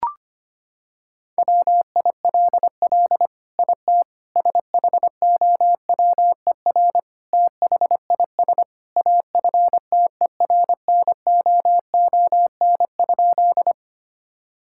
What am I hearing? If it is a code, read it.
Morse code, heard as a keyed tone, about 25 wpm, 700 Hz. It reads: WILL IT SHOWER THIS AFTERNOON?
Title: Sentences from Top 1600 Words 25wpm